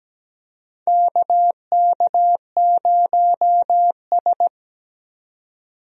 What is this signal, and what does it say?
Morse code, heard as a keyed tone, about 17 wpm, 700 Hz. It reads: KK0S
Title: USA Call Signs - 4 Characters - 17wpm